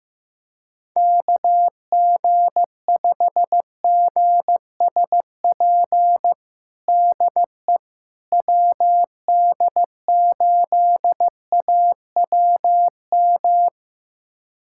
Morse code, heard as a keyed tone, about 15 wpm, 700 Hz. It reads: KG5GSP DE WD8AWM
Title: USA Callsign DE Callsign 15wpm